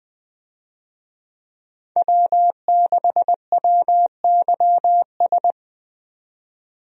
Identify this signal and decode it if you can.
Morse code, heard as a keyed tone, about 20 wpm, 700 Hz. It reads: W6WYS
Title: USA Call Signs - 5 Characters - 20wpm